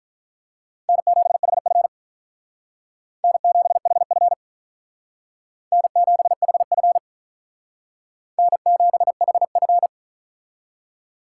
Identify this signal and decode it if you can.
Morse code, heard as a keyed tone, about 44 wpm, 700 Hz. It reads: N7HF N7HF N7HF N7HF